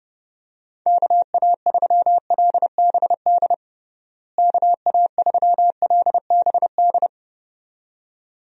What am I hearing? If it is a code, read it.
Morse code, heard as a keyed tone, about 29 wpm, 700 Hz. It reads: KA3LBD KA3LBD
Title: USA Call Signs - 6 Characters - Repeated 2x - 30wpm